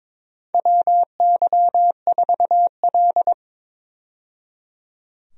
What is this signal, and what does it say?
Morse code, heard as a keyed tone, about 22 wpm, 700 Hz. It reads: WY4L
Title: USA Call Signs - 4 Characters - 22wpm